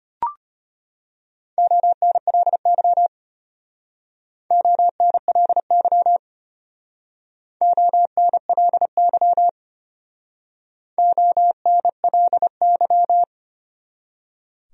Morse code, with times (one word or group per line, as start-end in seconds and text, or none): 1.6-3.1 ONLY
4.5-6.2 ONLY
7.6-9.5 ONLY
11.0-13.2 ONLY